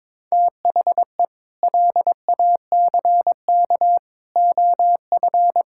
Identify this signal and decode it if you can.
Morse code, heard as a keyed tone, about 21 wpm, 700 Hz. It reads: THE LACK OF
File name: practice / 22wpm / Sentences with 3-Word Phrases - 22wpm